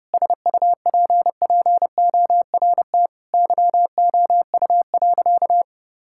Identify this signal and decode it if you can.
Morse code, heard as a keyed tone, about 30 wpm, 700 Hz. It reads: SUPPORT YOU.
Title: Sentences from Top 1000 Words - Review Part 5 - 30wpm